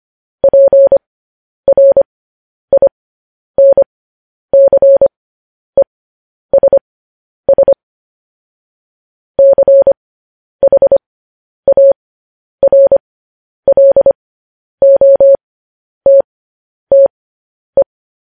Morse code, heard as a keyed tone, about 25 wpm, spaced as for 10 wpm, 550 Hz. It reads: PRINCESS CHARLOTTE
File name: podcast / News headlines-1x-25x10